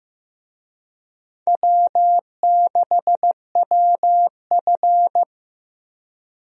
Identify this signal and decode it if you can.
Morse code, heard as a keyed tone, about 15 wpm, 700 Hz. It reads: W6WF
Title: USA Call Signs - 4 Characters - Repeated 2x - 15wpm